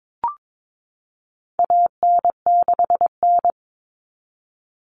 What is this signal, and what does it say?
Morse code, heard as a keyed tone, about 22 wpm, 700 Hz. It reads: AN6N
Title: Common Cut Numbers - 4-Digit 22wpm